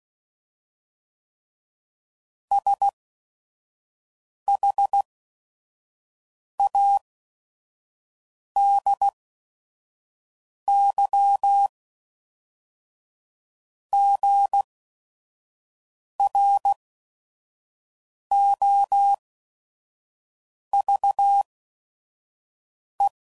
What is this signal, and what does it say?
Morse code, heard as a keyed tone, about 16 wpm, spaced as for 5 wpm, 800 Hz. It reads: SHADYGROVE
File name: CW Code Groups @5WPM / fromSM's / Cities